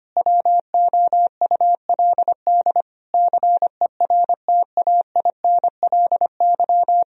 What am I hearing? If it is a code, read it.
Morse code, heard as a keyed tone, about 25 wpm, 700 Hz. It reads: WOULD CERTAINLY
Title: Sentences from Top 1000 Words - Review Part 3 - 25wpm